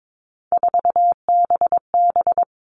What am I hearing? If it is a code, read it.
Morse code, heard as a keyed tone, about 22 wpm, 700 Hz. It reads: 4BB
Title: All Cut Numbers - 3-Digit 22wpm